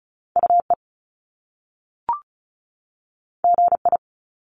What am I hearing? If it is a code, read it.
Morse code, heard as a keyed tone, about 36 wpm, 700 Hz. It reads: UE GI
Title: Top Two Letter Combinations in Words 35wpm